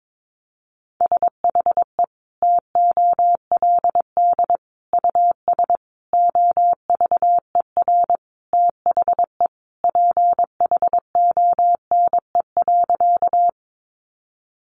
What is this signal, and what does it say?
Morse code, heard as a keyed tone, about 22 wpm, 700 Hz. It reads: SHE TOLD US OVER THE PHONE.
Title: Sentences from Top 1100 Words 22wpm